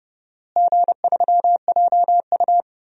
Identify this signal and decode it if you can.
Morse code, heard as a keyed tone, about 30 wpm, 700 Hz. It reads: G3JU
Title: DXCC-100 Call Signs - 4 Characters - Repeated 2x - 30wpm